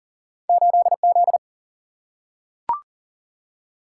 Morse code, heard as a keyed tone, about 40 wpm, 700 Hz. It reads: 8Z
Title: Single-Letter Single-Number 40wpm